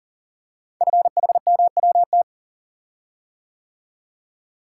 Morse code, heard as a keyed tone, about 40 wpm, 700 Hz. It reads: FHMWT
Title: Code Groups - 5 Letters - 40wpm